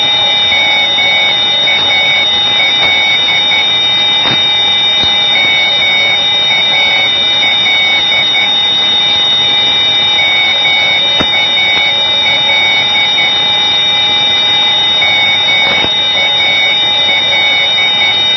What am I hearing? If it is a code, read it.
Morse code, heard as a keyed tone, about 10 wpm, 2250 Hz, with noise. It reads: MAL MAL MAL MAL
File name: ELB NDB 1989